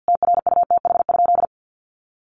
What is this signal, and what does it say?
Morse code, heard as a keyed tone, about 50 wpm, 700 Hz. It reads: TRUTH?
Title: Sentences from Top 1300 Words 50wpm